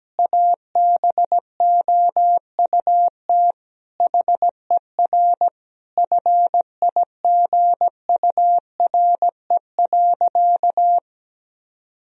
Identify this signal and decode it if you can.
Morse code, heard as a keyed tone, about 17 wpm, 700 Hz. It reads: ABOUT HER FIGURE.
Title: Sentences from Top 1000 Words - Review Part 2 - 17wpm